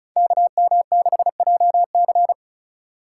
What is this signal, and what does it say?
Morse code, heard as a keyed tone, about 36 wpm, 700 Hz. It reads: KM6JC
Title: Code Groups - 5 Letter:Numbers - 35wpm